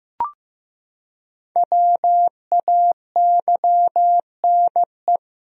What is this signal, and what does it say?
Morse code, heard as a keyed tone, about 15 wpm, 700 Hz. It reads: WAYNE
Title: USA Common Names - 15wpm